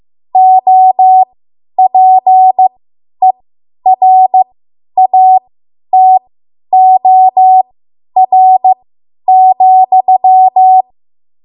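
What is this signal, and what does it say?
Morse code, heard as a keyed tone, about 15 wpm, spaced as for 10 wpm, 750 Hz. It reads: OPERATOR,
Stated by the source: Morse code practice run at 10 WPM from a code practice archive.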